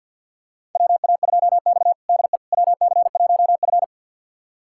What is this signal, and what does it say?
Morse code, heard as a keyed tone, about 48 wpm, 700 Hz. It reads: WA2X DE WC1F
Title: USA Callsign DE Callsign - 4 Characters - 50wpm